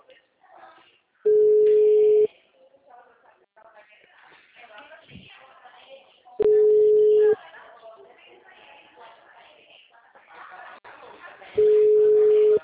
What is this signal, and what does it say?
Ringback tone (Europe, 425 Hz), with speech.